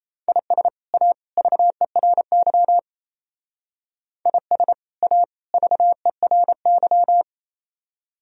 Morse code, heard as a keyed tone, about 37 wpm, 700 Hz. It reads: IS A VERY IS A VERY